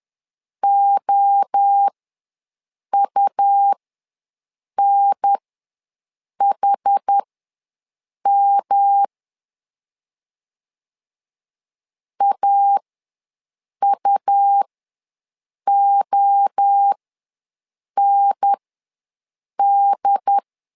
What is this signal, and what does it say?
Morse code, heard as a keyed tone, about 10 wpm, spaced as for 6 wpm, 800 Hz. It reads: OUNHM AUOND
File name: LEZIONI / 3 MWRO / 3 MWRO ws 10wpm 2310